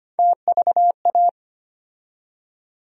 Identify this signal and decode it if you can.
Morse code, heard as a keyed tone, about 25 wpm, 700 Hz. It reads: TVA
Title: USA Call Sign Postfixes - 3 Letters 25wpm